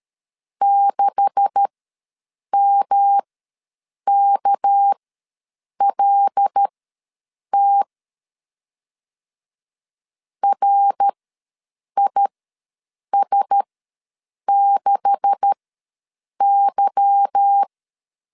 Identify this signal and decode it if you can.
Morse code, heard as a keyed tone, about 13 wpm, spaced as for 7 wpm, 800 Hz. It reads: 6MKLT RIS6Y
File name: LEZIONI / 9 286 ws 12wpm 3351